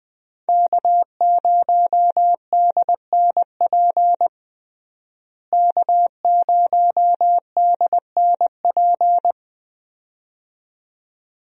Morse code, heard as a keyed tone, about 20 wpm, 700 Hz. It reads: K0DNP K0DNP
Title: USA Call Signs - 5 Characters - Repeated 2x - 20wpm